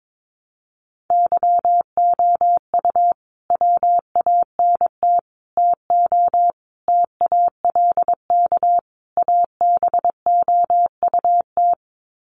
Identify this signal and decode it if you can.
Morse code, heard as a keyed tone, about 22 wpm, 700 Hz. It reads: YOU WANT TO TALK ABOUT
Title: Sets of 5 Words Encore 22wpm